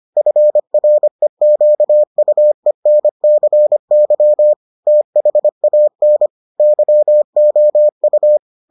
Morse code, heard as a keyed tone, about 25 wpm, 600 Hz. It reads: FREQUENCY THAN YOU